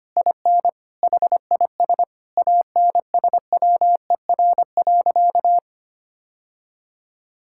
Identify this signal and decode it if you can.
Morse code, heard as a keyed tone, about 25 wpm, 700 Hz. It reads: IN HIS ANSWER.
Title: Sentences from Top 1600 Words 25wpm